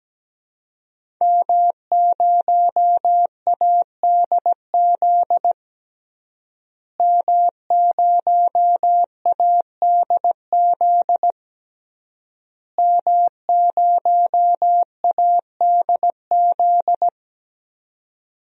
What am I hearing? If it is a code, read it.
Morse code, heard as a keyed tone, about 17 wpm, 700 Hz. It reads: M0ADZ M0ADZ M0ADZ
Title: DXCC-100 Call Signs - 5 Characters - Repeated 3x - 17wpm